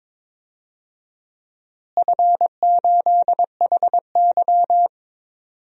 Morse code, heard as a keyed tone, about 22 wpm, 700 Hz. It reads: F8HY
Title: DXCC-100 Call Signs - 4 Characters - Repeated 3x - 22wpm